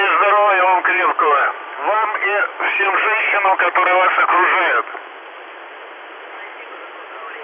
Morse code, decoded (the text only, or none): none